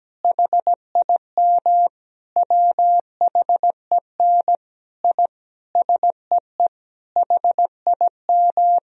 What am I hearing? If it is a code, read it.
Morse code, heard as a keyed tone, about 17 wpm, 700 Hz. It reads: HIM WHEN I SEE HIM
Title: Sentences from Top 1000 Words - Review Part 2 - 17wpm